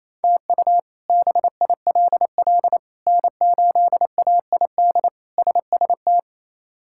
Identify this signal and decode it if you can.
Morse code, heard as a keyed tone, about 28 wpm, 700 Hz. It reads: TU BILL N8AID SST
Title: K1USN SST QSO Practice - 28wpm